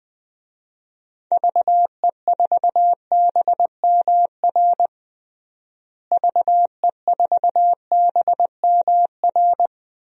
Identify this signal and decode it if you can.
Morse code, heard as a keyed tone, about 20 wpm, 700 Hz. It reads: VE4BMR VE4BMR
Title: DXCC-100 Call Signs - 6 Characters - Repeated 3x - 20wpm